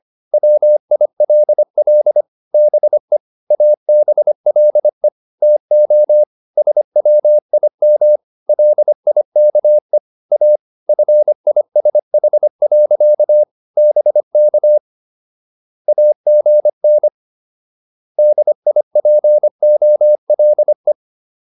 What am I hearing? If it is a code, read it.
Morse code, heard as a keyed tone, about 25 wpm, 600 Hz. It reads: WILL BE ABLE TO SWIM LIKE A FISH. BK AGN DIPOLE